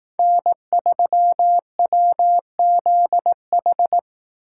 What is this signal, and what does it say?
Morse code, heard as a keyed tone, about 18 wpm, 700 Hz. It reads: N3WZH
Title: Speed Racing - USA 1x3 Call Signs 15wpm